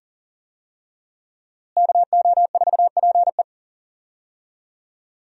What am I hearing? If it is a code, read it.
Morse code, heard as a keyed tone, about 40 wpm, 700 Hz. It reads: KO4PE
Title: USA Call Signs - 5 Characters - Repeated 2x - 40wpm